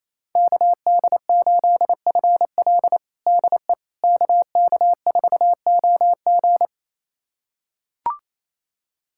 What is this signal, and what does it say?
Morse code, heard as a keyed tone, about 27 wpm, 700 Hz. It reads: KD8FL DE KK4OG
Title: USA Callsign DE Callsign - 5 Characters - 28wpm